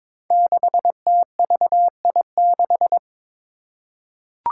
Morse code, heard as a keyed tone, about 22 wpm, 700 Hz. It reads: 6TVI6
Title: Five Challenging Letter-Number Combinations 22wpm